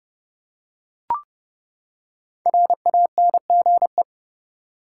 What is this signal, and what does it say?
Morse code, heard as a keyed tone, about 30 wpm, 700 Hz. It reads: RANGE
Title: Five Letter Words - 30wpm